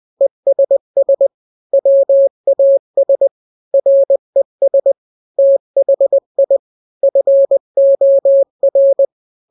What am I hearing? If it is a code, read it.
Morse code, heard as a keyed tone, about 20 wpm, 550 Hz. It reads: ESS WAS RES THI FOR
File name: n-grams / 3-grams 20 18 WPM